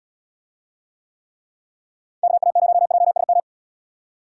Morse code, heard as a keyed tone, about 76 wpm, 700 Hz. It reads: KA0JSM